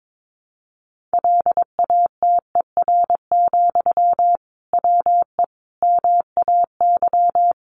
Morse code, heard as a keyed tone, about 22 wpm, 700 Hz. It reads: LATER, WE MAY
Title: Sentences from Top 800 Words 22wpm